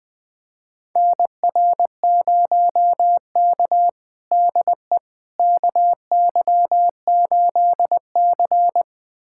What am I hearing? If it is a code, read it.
Morse code, heard as a keyed tone, about 20 wpm, 700 Hz. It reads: NR0K DE KY8C
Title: USA Callsign DE Callsign - 4 Characters - 20wpm